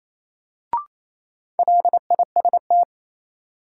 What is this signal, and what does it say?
Morse code, heard as a keyed tone, about 28 wpm, 700 Hz. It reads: LIST
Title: Sentences from Top 400 Words - New Word Drill - 28wpm